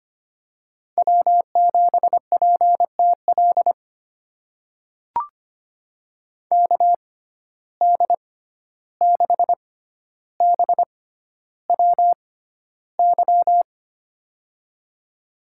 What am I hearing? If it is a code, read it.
Morse code, heard as a keyed tone, about 25 wpm, spaced as for 12 wpm, 700 Hz. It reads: W7PTL KD6BWY
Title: USA Call Signs - 6x Character Spacing - 25wpm